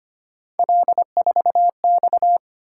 Morse code, heard as a keyed tone, about 25 wpm, 700 Hz. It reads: L4X
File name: Three Letter-Number Combinations - 25wpm